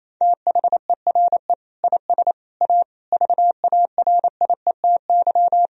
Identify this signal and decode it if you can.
Morse code, heard as a keyed tone, about 28 wpm, 700 Hz. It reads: THERE IS A VARIETY
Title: Sentences from Top 1000 Words - Review Part 4 - 28wpm